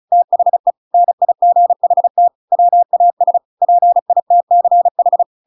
Morse code, heard as a keyed tone, about 36 wpm, 700 Hz. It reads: THE NIGHT WAS PITCH
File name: Pinocchio / Pinocchio ch6 35 35 ik1hge